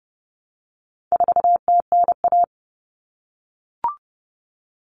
Morse code, heard as a keyed tone, about 30 wpm, 700 Hz. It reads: 4TNA
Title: Common Cut Numbers - 4-Digit 30wpm